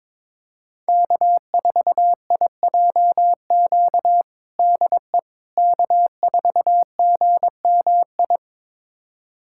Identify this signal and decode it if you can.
Morse code, heard as a keyed tone, about 22 wpm, 700 Hz. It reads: K4IJQ DE K4GMI
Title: USA Callsign DE Callsign 22wpm